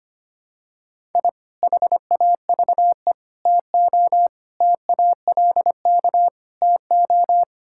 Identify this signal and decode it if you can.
Morse code, heard as a keyed tone, about 25 wpm, 700 Hz. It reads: I HAVE TO TALK TO
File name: practice / 25wpm / Sentences with 3-Word Phrases - 25wpm